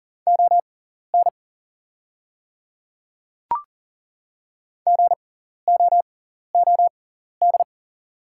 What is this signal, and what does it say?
Morse code, heard as a keyed tone, about 40 wpm, spaced as for 14 wpm, 700 Hz. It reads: ON GOOD
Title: 6x Character Spacing - Top 200 Words - 40wpm